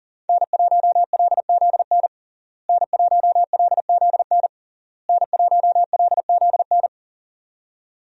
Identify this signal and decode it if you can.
Morse code, heard as a keyed tone, about 37 wpm, 700 Hz. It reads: N1LZN N1LZN N1LZN
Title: USA Call Signs - 5 Characters - Repeated 3x - 40wpm